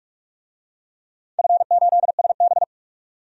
Morse code, heard as a keyed tone, about 47 wpm, 700 Hz. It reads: F8SB
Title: DXCC-100 Call Signs - 4 Characters - 45wpm